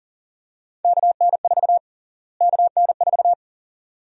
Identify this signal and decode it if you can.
Morse code, heard as a keyed tone, about 38 wpm, 700 Hz. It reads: KN4 KN4
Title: USA Call Sign Prefixes - Repeated 2x - 40wpm